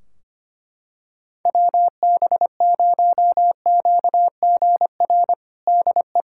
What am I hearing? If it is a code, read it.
Morse code, heard as a keyed tone, about 25 wpm, 700 Hz. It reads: WB0QGR DE
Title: Practice QSO - Exchange 4 - Ending - 25wpm